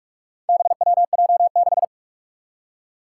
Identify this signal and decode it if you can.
Morse code, heard as a keyed tone, about 44 wpm, 700 Hz. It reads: BWJ6
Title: Four Challenging Letter-Number Combinations 45wpm